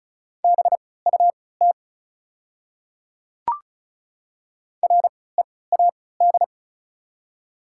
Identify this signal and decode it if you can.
Morse code, heard as a keyed tone, about 34 wpm, spaced as for 20 wpm, 700 Hz. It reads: BUT READ
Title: 3x Character Spacing - Top 300 Words - 35wpm